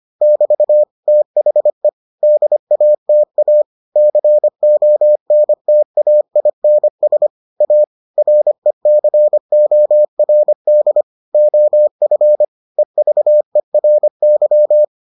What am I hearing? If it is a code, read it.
Morse code, heard as a keyed tone, about 25 wpm, 600 Hz. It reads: <BT> THE DATA CONTAINS A RECORD OF EVERY